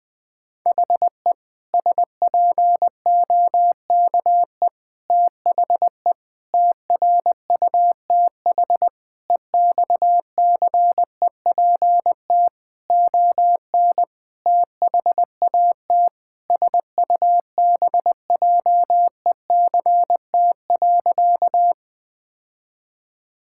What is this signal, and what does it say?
Morse code, heard as a keyed tone, about 19 wpm, 700 Hz. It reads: HE SPOKE THE TRUTH EXCEPT ON THAT SUBJECT.
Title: Sentences from Top 1600 Words 20wpm